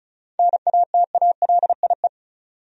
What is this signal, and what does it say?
Morse code, heard as a keyed tone, about 34 wpm, 700 Hz. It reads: NATALIE
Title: USA Common Names - 35wpm